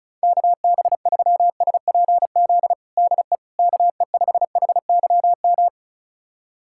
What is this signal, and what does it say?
Morse code, heard as a keyed tone, about 36 wpm, 700 Hz. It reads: KB3SPZ DE KE5HYM
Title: USA Callsign DE Callsign - 6 Characters - 35wpm